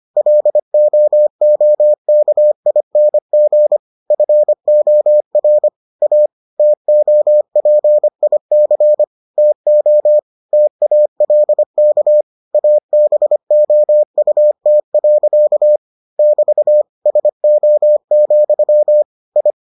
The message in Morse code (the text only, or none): LOOKING FOR A TOPIC TO TALK ABOUT. <BT> SO, I